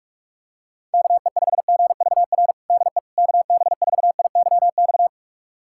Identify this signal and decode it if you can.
Morse code, heard as a keyed tone, about 47 wpm, 690 Hz. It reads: KE5GVR DE KB4IYX
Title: USA Callsign DE Callsign 45wpm